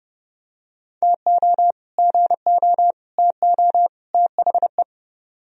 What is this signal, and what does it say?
Morse code, heard as a keyed tone, about 31 wpm, 700 Hz. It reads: TO GO TO THE